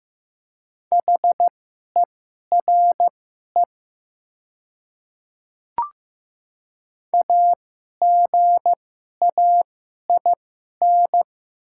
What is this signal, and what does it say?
Morse code, heard as a keyed tone, about 15 wpm, spaced as for 11 wpm, 700 Hz. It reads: HERE AGAIN
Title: 2x Character Spacing - Top 200 Words - 15wpm